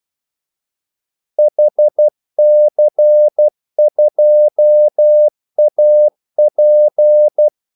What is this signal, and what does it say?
Morse code, heard as a keyed tone, about 12 wpm, 600 Hz. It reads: HC2AP
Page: 2 ekstra ordmellemrum mellem kaldesignalerne